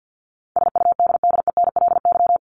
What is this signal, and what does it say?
Morse code, heard as a keyed tone, about 50 wpm, 700 Hz. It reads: SUDDENLY
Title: LY Suffix - Top 100 Words with Base Words 50wpm